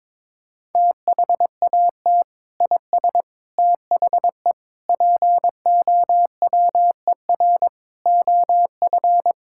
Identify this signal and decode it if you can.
Morse code, heard as a keyed tone, about 22 wpm, 700 Hz. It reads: THAT IS THE POWER OF
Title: Sentences from Top 1000 Words - Review Part 5 - 22wpm